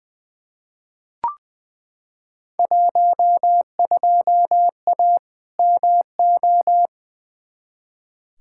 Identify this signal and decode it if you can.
Morse code, heard as a keyed tone, about 20 wpm, 700 Hz. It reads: 12A MO
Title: Field Day Exchanges Encore - 20wpm